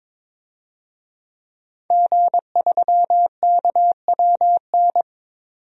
Morse code, heard as a keyed tone, about 22 wpm, 700 Hz. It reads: G3KWN
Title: DXCC-100 Call Signs - 5 Characters - 22wpm